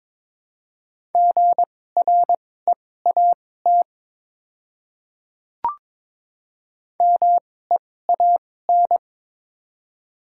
Morse code, heard as a keyed tone, about 23 wpm, spaced as for 16 wpm, 700 Hz. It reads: GREAT MEAN
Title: 2x Character Spacing - Top 300 Words - 22wpm